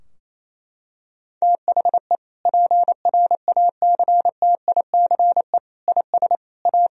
Morse code, heard as a keyed tone, about 28 wpm, 700 Hz. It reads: THE PRACTICE IS A
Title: Sentences from Top 1000 Words - Review Part 4 - 28wpm